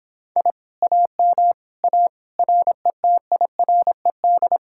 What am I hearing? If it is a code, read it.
Morse code, heard as a keyed tone, about 26 wpm, 700 Hz. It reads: I AM A RETIRED